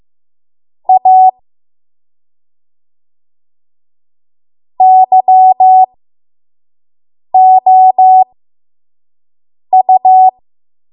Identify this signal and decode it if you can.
Morse code, heard as a keyed tone, about 15 wpm, spaced as for 5 wpm, 750 Hz. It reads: A YOU
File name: Morse / 05WPM